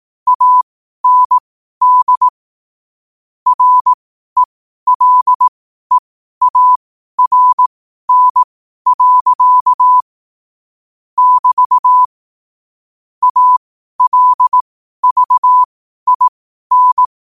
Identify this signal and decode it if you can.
Morse code, heard as a keyed tone, about 18 wpm, spaced as for 12 wpm, 1000 Hz. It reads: AND RELEARN. <BT> ALVIN